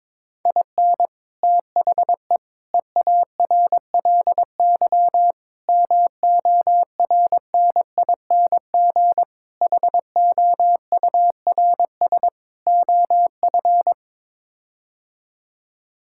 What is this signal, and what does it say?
Morse code, heard as a keyed tone, about 22 wpm, 700 Hz. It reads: IN THE EARLY MORNING HOURS OF
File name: practice / 22wpm / Sets of 6 Words - 22wpm